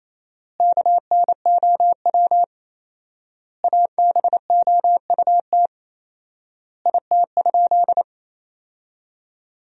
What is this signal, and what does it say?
Morse code, heard as a keyed tone, about 28 wpm, 700 Hz. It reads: KNOW ABOUT IT?
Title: 4x Word-Spacing - Sentences from Top 100 Words - 28wpm